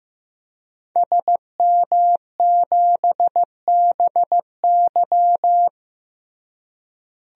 Morse code, heard as a keyed tone, about 16 wpm, 700 Hz. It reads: SM7BY
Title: DXCC-100 Call Signs - 5 Characters - Repeated 2x - 15wpm